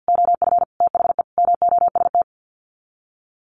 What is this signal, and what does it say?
Morse code, heard as a keyed tone, about 48 wpm, 700 Hz. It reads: OF THE MOST
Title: Sets of 5 Words Encore 50wpm